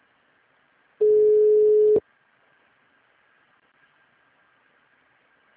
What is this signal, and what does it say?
Ringback tone (Europe, 425 Hz).